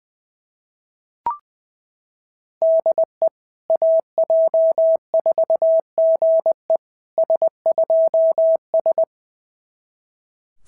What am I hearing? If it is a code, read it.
Morse code, heard as a keyed tone, about 21 wpm, 690 Hz. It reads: DE AJ4GE S2S
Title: SOTA QSO Practice 20wpm